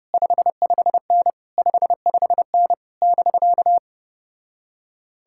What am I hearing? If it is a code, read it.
Morse code, heard as a keyed tone, about 33 wpm, 700 Hz. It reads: 55N 55N <BK>